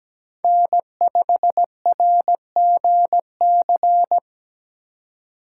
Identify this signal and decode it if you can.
Morse code, heard as a keyed tone, about 17 wpm, 700 Hz. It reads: N5RGC
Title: USA Call Signs - 5 Characters - 4x Character Spacing - 17wpm